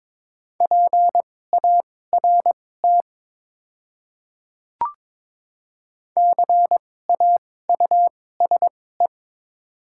Morse code, heard as a keyed tone, about 22 wpm, spaced as for 16 wpm, 700 Hz. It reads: PART CAUSE
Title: 2x Character Spacing - Top 500 Words - 22wpm